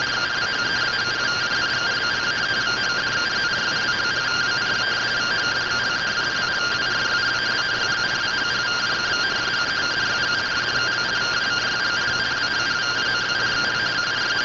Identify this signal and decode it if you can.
Two-tone teleprinter signal, about 170 Hz shift.
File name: findings / SAGITAL fr15 nb203 3mm.gen-41 rtty rx